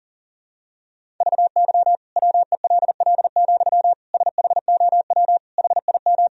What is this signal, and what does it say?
Morse code, heard as a keyed tone, about 37 wpm, 700 Hz. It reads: VY WELL, SHOW HIM
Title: Sentences from Top 200 Words and Abbreviations 40wpm